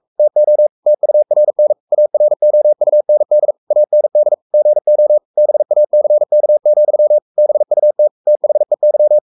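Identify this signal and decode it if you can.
Morse code, heard as a keyed tone, about 44 wpm, 600 Hz. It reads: TO TURN AROUND AND GO BACK, BUT THEY
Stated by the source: Morse code at 43 wpm sending Ken Follett's novel Never in English, from CW med Gnister.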